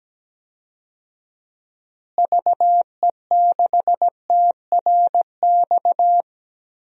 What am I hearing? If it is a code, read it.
Morse code, heard as a keyed tone, about 17 wpm, 700 Hz. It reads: VE6TRX
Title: DXCC-100 Call Signs - 6 Characters - Repeated 2x - 17wpm